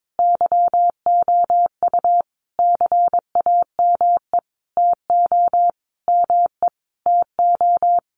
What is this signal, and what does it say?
Morse code, heard as a keyed tone, about 22 wpm, 700 Hz. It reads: YOU CAME TO ME TO
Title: Sentences from Top 200 Words 22wpm